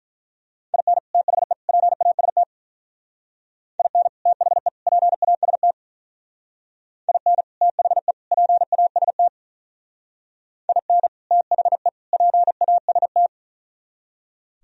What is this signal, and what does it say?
Morse code, heard as a keyed tone, about 44 wpm, 700 Hz. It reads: IN THE PAST IN THE PAST IN THE PAST IN THE PAST